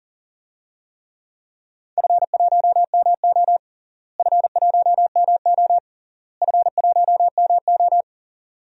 Morse code, heard as a keyed tone, about 40 wpm, 700 Hz. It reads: F1MO F1MO F1MO
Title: DXCC-100 Call Signs - 4 Characters - Repeated 3x - 40wpm